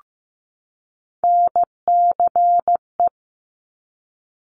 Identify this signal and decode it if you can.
Morse code, heard as a keyed tone, about 15 wpm, 700 Hz. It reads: NCE
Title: Top Three Letter Combinations in Words 15wpm